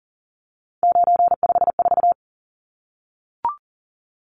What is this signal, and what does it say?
Morse code, heard as a keyed tone, about 37 wpm, 700 Hz. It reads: 954
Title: Numbers - 3-Digit 40wpm